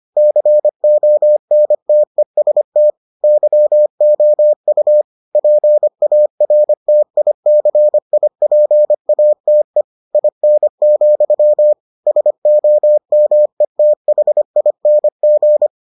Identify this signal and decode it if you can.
Morse code, heard as a keyed tone, about 25 wpm, 600 Hz. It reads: CONTEST YOU PARTICIPATE IN, SOMETHING